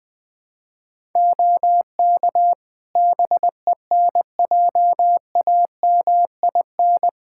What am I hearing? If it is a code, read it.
Morse code, heard as a keyed tone, about 20 wpm, 700 Hz. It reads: OK BENJAMIN
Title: Practice QSO - Exchange 4 - Ending - 20wpm